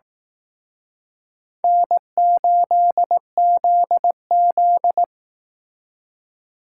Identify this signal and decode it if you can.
Morse code, heard as a keyed tone, about 18 wpm, 700 Hz. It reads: N8ZZ
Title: Speed Racing - USA 1x2 Call Signs 15wpm